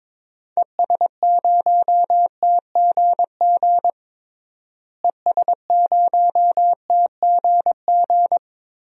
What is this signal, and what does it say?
Morse code, heard as a keyed tone, about 22 wpm, 700 Hz. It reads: ES0TGG ES0TGG
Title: DXCC-100 Call Signs - 6 Characters - Repeated 3x - 22wpm